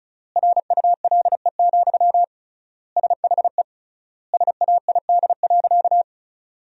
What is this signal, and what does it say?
Morse code, heard as a keyed tone, about 36 wpm, 700 Hz. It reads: RULE, SHE SAID.
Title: 3x Word-Spacing - Sentences from Top 400 Words - 35wpm